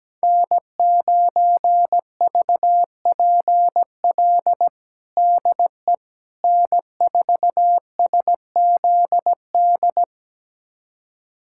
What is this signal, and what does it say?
Morse code, heard as a keyed tone, about 17 wpm, 700 Hz. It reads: N9VPL DE N4SZD
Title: USA Callsign DE Callsign - 5 Characters - 17wpm